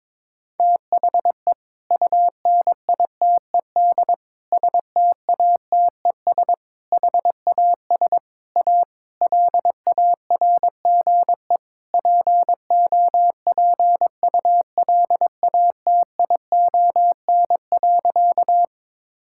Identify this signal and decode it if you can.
Morse code, heard as a keyed tone, about 22 wpm, 700 Hz. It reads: THE UNITED STATES HAS A LARGE POPULATION.
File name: practice / 22wpm / Sentences with 4-Word Phrases - 22wpm